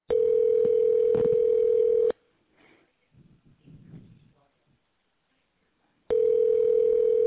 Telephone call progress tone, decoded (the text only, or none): ringback tone